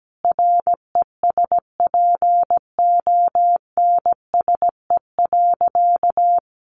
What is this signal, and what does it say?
Morse code, heard as a keyed tone, about 17 wpm, 700 Hz. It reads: RESPONSE.
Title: Sentences from Top 1400 Words 17wpm